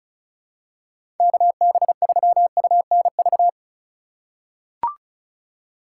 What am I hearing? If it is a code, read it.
Morse code, heard as a keyed tone, about 36 wpm, 700 Hz. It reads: KB3UNV
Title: USA Call Signs - 6 Characters - 4x Character Spacing - 35wpm